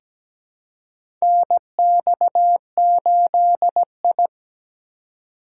Morse code, heard as a keyed tone, about 17 wpm, 700 Hz. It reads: NX8I
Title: USA Call Signs - 4 Characters - 3x Character Spacing - 17wpm